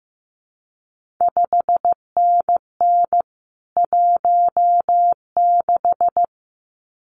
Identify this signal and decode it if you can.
Morse code, heard as a keyed tone, about 15 wpm, 700 Hz. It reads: 5NN 16
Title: CQ WW DX Contest Exchange - 15wpm